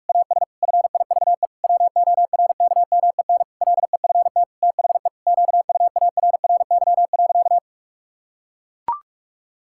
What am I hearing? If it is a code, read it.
Morse code, heard as a keyed tone, about 41 wpm, 700 Hz. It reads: AS FIVE WORKMEN LEFT THE QUARRY.
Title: Pangram Challenge v2 - 45wpm